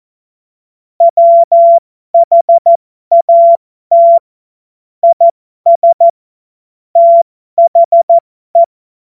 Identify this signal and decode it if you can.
Morse code, heard as a keyed tone, about 14 wpm, spaced as for 12 wpm, 680 Hz. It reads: WHAT IS THE
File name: Morsekurs / Witze